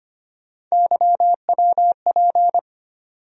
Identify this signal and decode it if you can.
Morse code, heard as a keyed tone, about 25 wpm, 700 Hz. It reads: YWP
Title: USA Call Sign Postfixes - 3 Letters 25wpm